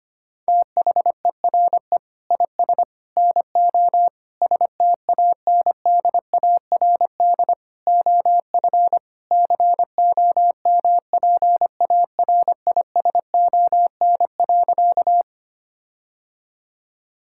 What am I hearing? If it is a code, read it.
Morse code, heard as a keyed tone, about 25 wpm, 700 Hz. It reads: THERE IS NO STANDARD OF COMPARISON.
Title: Sentences from Top 1600 Words 25wpm